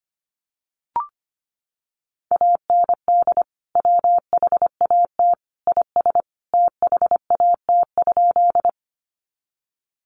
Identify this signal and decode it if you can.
Morse code, heard as a keyed tone, about 25 wpm, 700 Hz. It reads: AND WHAT IS THAT?
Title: Sentences from Top 100 Words 25wpm